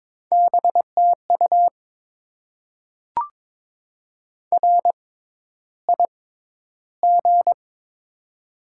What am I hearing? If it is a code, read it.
Morse code, heard as a keyed tone, about 23 wpm, spaced as for 11 wpm, 700 Hz. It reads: BTU RIG
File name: Top 20 QSO Elements - 6x Character Spacing - 22wpm